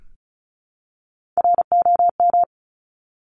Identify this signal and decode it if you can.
Morse code, heard as a keyed tone, about 36 wpm, 700 Hz. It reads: ROM
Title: Top Three Letter Combinations in Words 35wpm